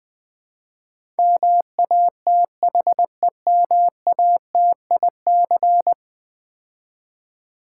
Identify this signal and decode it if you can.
Morse code, heard as a keyed tone, about 20 wpm, 700 Hz. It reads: MATHEMATIC
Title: Build A Word - 5 Letters At A Time - 12-Letter Words - 20wpm